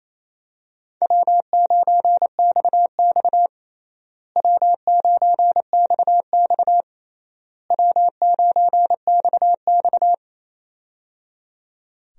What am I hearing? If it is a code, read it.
Morse code, heard as a keyed tone, about 28 wpm, 700 Hz. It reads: W9XX W9XX W9XX
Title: USA Call Signs - 4 Characters - Repeated 3x - 28wpm